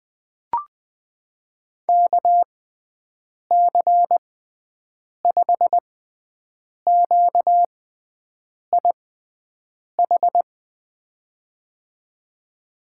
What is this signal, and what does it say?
Morse code, heard as a keyed tone, about 21 wpm, spaced as for 7 wpm, 700 Hz. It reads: KC5QIH
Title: USA Call Signs - 6 Characters - 6x Character Spacing - 20wpm